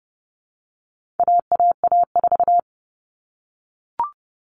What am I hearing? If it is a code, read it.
Morse code, heard as a keyed tone, about 30 wpm, 700 Hz. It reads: AAA4
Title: All Cut Numbers - 4-Digit 30wpm